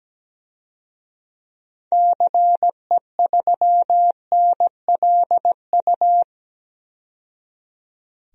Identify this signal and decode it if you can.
Morse code, heard as a keyed tone, about 17 wpm, 700 Hz. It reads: CE3NLU
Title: DXCC-100 Call Signs - 6 Characters - 17wpm